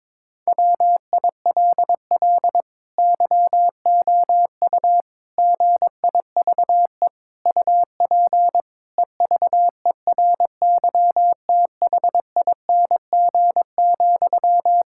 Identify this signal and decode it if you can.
Morse code, heard as a keyed tone, about 22 wpm, 700 Hz. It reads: WILL YOU GIVE UP EVERYTHING,
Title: Sentences from Top 1000 Words - Review Part 5 - 22wpm